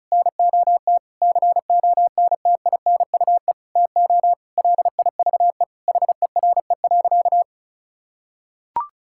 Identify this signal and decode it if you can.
Morse code, heard as a keyed tone, about 36 wpm, 700 Hz. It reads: NOT CONTINUE TO LIVE HERE.
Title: Sentences from Top 1000 Words - Review Part 6 - 35wpm